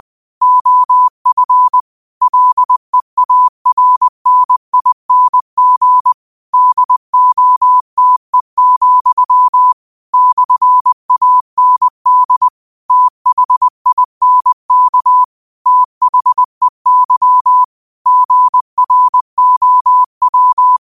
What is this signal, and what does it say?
Morse code, heard as a keyed tone, about 20 wpm, 1000 Hz. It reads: OF LEARNING DOTE, /AND THINK THEY GROW